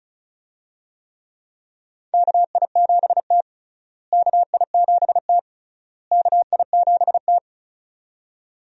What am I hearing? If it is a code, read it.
Morse code, heard as a keyed tone, about 36 wpm, 700 Hz. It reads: KI7T KI7T KI7T
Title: USA Call Signs - 4 Characters - Repeated 3x - 35wpm